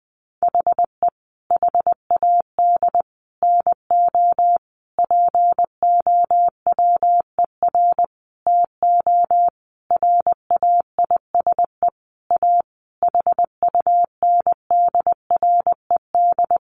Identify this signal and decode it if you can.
Morse code, heard as a keyed tone, about 20 wpm, 700 Hz. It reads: HE HAD NO POWER TO RAISE A HUNDRED
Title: Sentences from Top 1400 Words 20wpm